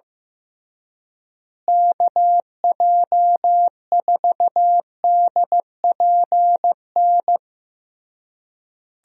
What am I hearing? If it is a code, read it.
Morse code, heard as a keyed tone, about 15 wpm, 700 Hz. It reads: KJ4DPN
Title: USA Call Signs - 6 Characters - Repeated 3x - 15wpm